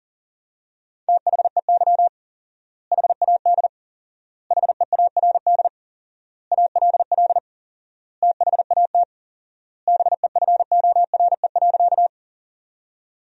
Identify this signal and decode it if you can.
Morse code, heard as a keyed tone, about 40 wpm, 700 Hz. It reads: THEY HAD HEARD ALL THAT BEFORE.
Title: 4x Word-Spacing - Sentences from Top 400 Words - 40wpm